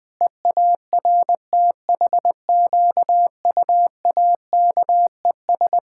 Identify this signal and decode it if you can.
Morse code, heard as a keyed tone, about 20 wpm, 700 Hz. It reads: EARTHQUAKES
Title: Pangram Challenge v2 - 20wpm